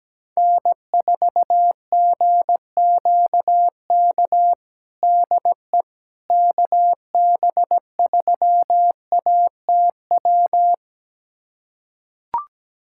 Morse code, heard as a keyed tone, about 17 wpm, 700 Hz. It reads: N4GQK DE KB3ATW
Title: USA Callsign DE Callsign 17wpm